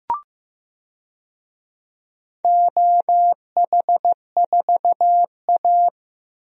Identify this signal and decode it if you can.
Morse code, heard as a keyed tone, about 15 wpm, 700 Hz. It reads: OH4A
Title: DXCC-100 Call Signs - 4 Characters - Repeated 3x - 15wpm